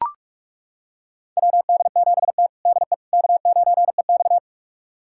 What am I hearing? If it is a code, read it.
Morse code, heard as a keyed tone, about 41 wpm, 700 Hz. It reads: WD7T DE K9EX
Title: USA Callsign DE Callsign - 4 Characters - 45wpm